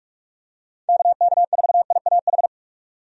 Morse code, heard as a keyed tone, about 47 wpm, 700 Hz. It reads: KK4IAH
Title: USA Call Signs - 6 Characters - 45wpm